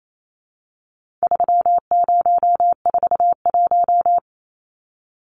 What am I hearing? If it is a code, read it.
Morse code, heard as a keyed tone, about 28 wpm, 700 Hz. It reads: 3041
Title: Numbers - 4-Digit 28wpm